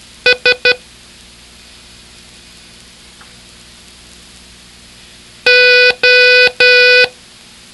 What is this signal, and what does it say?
Morse code, heard as a keyed tone, about 11 wpm, 500 Hz. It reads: S O